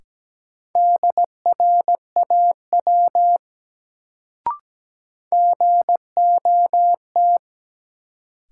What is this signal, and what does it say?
Morse code, heard as a keyed tone, about 17 wpm, 700 Hz. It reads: DRAW GOT
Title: Sentences from Top 300 Words - New Word Drill - 17wpm